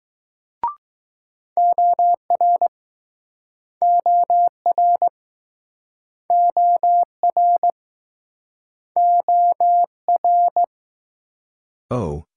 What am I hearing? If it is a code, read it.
Morse code, heard as a keyed tone, about 22 wpm, 700 Hz. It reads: OR OR OR OR